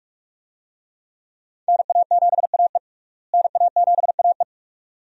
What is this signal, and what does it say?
Morse code, heard as a keyed tone, about 41 wpm, 700 Hz. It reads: NA7AE NA7AE
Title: USA Call Signs - 5 Characters - Repeated 2x - 45wpm